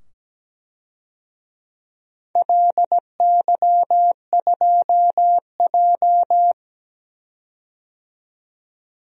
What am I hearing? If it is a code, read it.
Morse code, heard as a keyed tone, about 17 wpm, 700 Hz. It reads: LY2J
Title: DXCC-100 Call Signs - 4 Characters - 17wpm